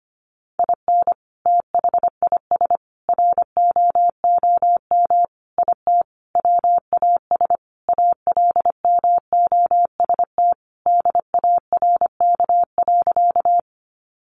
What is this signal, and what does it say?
Morse code, heard as a keyed tone, about 25 wpm, 700 Hz. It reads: IN THIS ROOM IT WAS ALMOST DARK.
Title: Sentences from Top 700 Words 25wpm